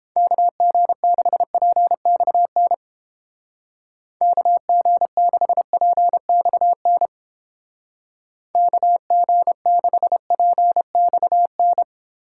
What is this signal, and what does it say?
Morse code, heard as a keyed tone, about 29 wpm, 700 Hz. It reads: KG6PXN KG6PXN KG6PXN